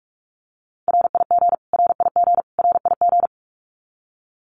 Morse code, heard as a keyed tone, about 44 wpm, 700 Hz. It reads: RIG RIG RIG
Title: Abbreviations Repeated 3x - 45wpm